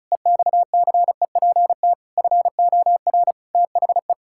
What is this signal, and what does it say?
Morse code, heard as a keyed tone, about 34 wpm, 700 Hz. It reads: EXCEPT FOR THE
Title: Pangram Challenge v2 - 35wpm